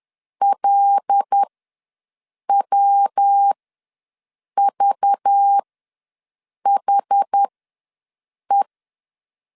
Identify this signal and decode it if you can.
Morse code, heard as a keyed tone, about 10 wpm, spaced as for 6 wpm, 800 Hz. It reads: LWVHE
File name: LEZIONI / 5 LBVC / 5 LBV ws 10wpm 3645